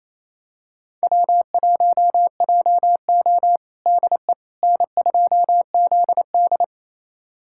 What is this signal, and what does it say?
Morse code, heard as a keyed tone, about 28 wpm, 700 Hz. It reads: W1JO DE N2ZD
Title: USA Callsign DE Callsign - 4 Characters - 28wpm